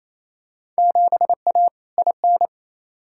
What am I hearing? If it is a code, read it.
Morse code, heard as a keyed tone, about 27 wpm, 700 Hz. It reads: 7A IN
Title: Field Day Exchanges Encore - 28wpm